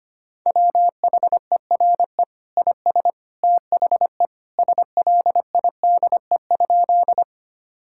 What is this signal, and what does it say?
Morse code, heard as a keyed tone, about 25 wpm, 700 Hz. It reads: WHERE IS THE SLIDE?
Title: Sentences from Top 1600 Words 25wpm